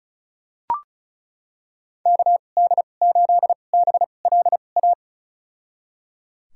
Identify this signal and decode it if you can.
Morse code, heard as a keyed tone, about 34 wpm, spaced as for 25 wpm, 700 Hz. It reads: KD8BLA
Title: USA Call Signs - 6 Characters - 2x Character Spacing - 35wpm